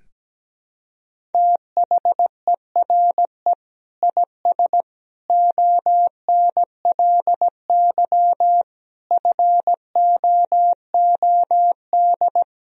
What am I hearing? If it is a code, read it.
Morse code, heard as a keyed tone, about 17 wpm, 700 Hz. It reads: THERE IS ONLY FOOD
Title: Sentences from Top 300 Words and Abbreviations 17wpm